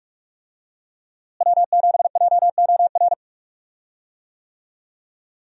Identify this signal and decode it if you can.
Morse code, heard as a keyed tone, about 46 wpm, 700 Hz. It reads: W7JOR
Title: USA Call Signs - 5 Characters - Repeated 2x - 45wpm